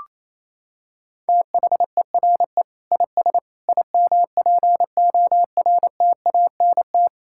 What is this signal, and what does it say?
Morse code, heard as a keyed tone, about 28 wpm, 700 Hz. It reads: THERE IS IMPORTANT
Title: Sentences from Top 1000 Words - Review Part 6 - 28wpm